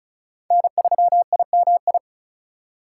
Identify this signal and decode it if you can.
Morse code, heard as a keyed tone, about 36 wpm, 700 Hz. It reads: N3IMI
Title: USA Call Signs - 5 Characters - 3x Character Spacing - 35wpm